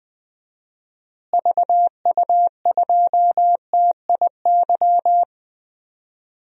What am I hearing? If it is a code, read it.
Morse code, heard as a keyed tone, about 20 wpm, 700 Hz. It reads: VU2TIY
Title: DXCC-100 Call Signs - 6 Characters - Repeated 3x - 20wpm